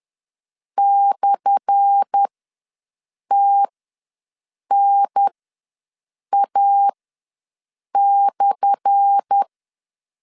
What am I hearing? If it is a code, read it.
Morse code, heard as a keyed tone, about 10 wpm, spaced as for 6 wpm, 800 Hz. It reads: /TNA/
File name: LEZIONI / 10 SIGNS / 10A SLASH ws 10wpm 4121